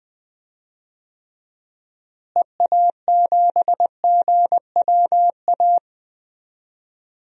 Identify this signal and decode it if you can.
Morse code, heard as a keyed tone, about 20 wpm, 700 Hz. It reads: EA7GWA
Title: DXCC-100 Call Signs - 6 Characters - Repeated 3x - 20wpm